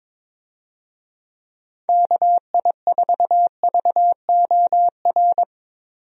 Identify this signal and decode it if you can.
Morse code, heard as a keyed tone, about 22 wpm, 700 Hz. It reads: KI4VOR
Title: USA Call Signs - 6 Characters - 22wpm